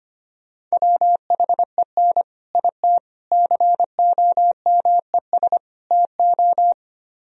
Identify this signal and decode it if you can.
Morse code, heard as a keyed tone, about 31 wpm, 700 Hz. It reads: WHEN IT COMES TO
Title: Speed-Racing - Sets of 4 Words 25wpm